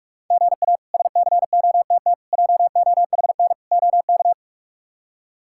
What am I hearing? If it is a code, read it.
Morse code, heard as a keyed tone, about 46 wpm, 700 Hz. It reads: GA SCOTT JOHN OK
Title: K1USN SST QSO Practice - 45wpm